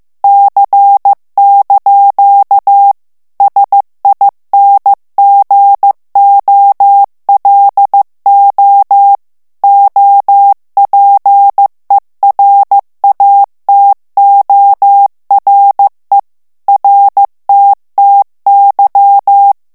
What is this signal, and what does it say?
Morse code, heard as a keyed tone, about 15 wpm, 800 Hz. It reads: C) SINGOLO OPERATORE RTTY
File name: morse mp3 / ITALIEN texte clair 2 vitesse 14 mots minute